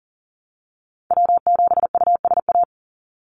Morse code, heard as a keyed tone, about 38 wpm, 700 Hz. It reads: W7USA
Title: NA SOTA Activator Callsigns 40wpm